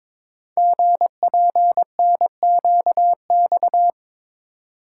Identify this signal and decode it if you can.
Morse code, heard as a keyed tone, about 22 wpm, 700 Hz. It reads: GPNQX
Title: Code Groups - 5 Letter:Numbers - 22wpm